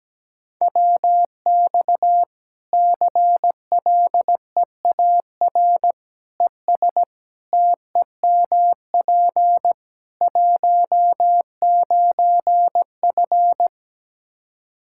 Morse code, heard as a keyed tone, about 17 wpm, 700 Hz. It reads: WX CLEAR ES TEMP 19F
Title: Practice QSO - Exchange 2 - Info - 17wpm